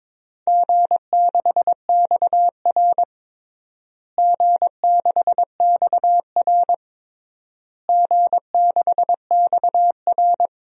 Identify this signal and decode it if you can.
Morse code, heard as a keyed tone, about 22 wpm, 700 Hz. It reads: G6XR G6XR G6XR
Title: DXCC-100 Call Signs - 4 Characters - Repeated 3x - 22wpm